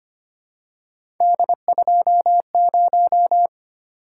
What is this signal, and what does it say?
Morse code, heard as a keyed tone, about 25 wpm, 700 Hz. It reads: D20
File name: Three Letter-Number Combinations - 25wpm